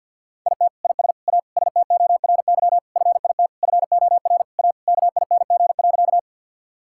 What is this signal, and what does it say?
Morse code, heard as a keyed tone, about 50 wpm, 690 Hz. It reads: IT IS A STORY FIT FOR A KING.
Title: Sentences from Top 1000 Words - Review Part 5 - 50wpm